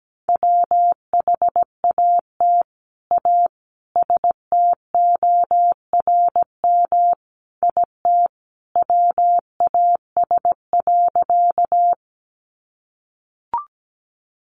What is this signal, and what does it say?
Morse code, heard as a keyed tone, about 17 wpm, 700 Hz. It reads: WHAT A STORM IT WAS.
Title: Sentences from Top 1500 Words 17wpm